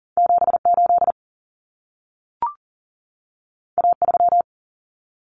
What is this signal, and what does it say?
Morse code, heard as a keyed tone, about 40 wpm, 700 Hz. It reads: 78 A3
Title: Common Cut Numbers - 2-Digit 40wpm